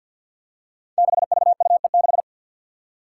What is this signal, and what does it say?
Morse code, heard as a keyed tone, about 48 wpm, 700 Hz. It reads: 6VUE6
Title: Five Challenging Letter-Number Combinations 50wpm